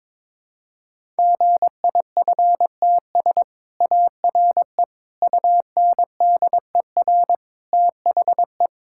Morse code, heard as a keyed tone, about 22 wpm, 700 Hz. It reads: GIFTS ARE UNDER THE
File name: practice / 22wpm / Easy Sentences with Hints - 22wpm